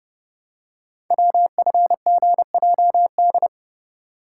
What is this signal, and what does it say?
Morse code, heard as a keyed tone, about 30 wpm, 700 Hz. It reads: WFGJD
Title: Code Groups - 5 Letter:Numbers - 30wpm